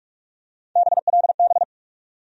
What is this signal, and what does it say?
Morse code, heard as a keyed tone, about 42 wpm, 700 Hz. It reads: BLB
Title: USA Call Sign Postfixes - 3 Letters 45wpm